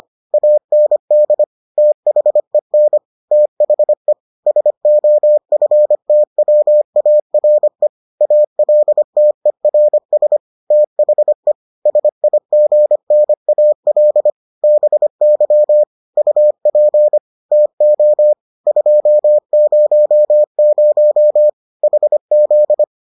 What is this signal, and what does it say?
Morse code, heard as a keyed tone, about 25 wpm, 600 Hz. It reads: AND THEN THE SOFTWARE ALTERS THE SIGNAL BY UP TO 200 HZ